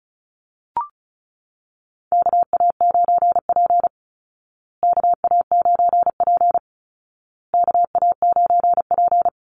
Morse code, heard as a keyed tone, about 34 wpm, 700 Hz. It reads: KA9P KA9P KA9P
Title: Call Signs Repeated 2x - 35wpm